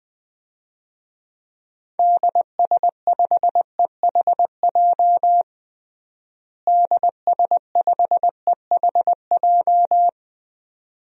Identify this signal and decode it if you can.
Morse code, heard as a keyed tone, about 21 wpm, 700 Hz. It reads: DS5EHJ DS5EHJ
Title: DXCC-100 Call Signs - 6 Characters - Repeated 2x - 20wpm